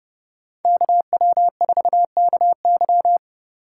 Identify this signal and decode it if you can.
Morse code, heard as a keyed tone, about 28 wpm, 700 Hz. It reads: KW4KY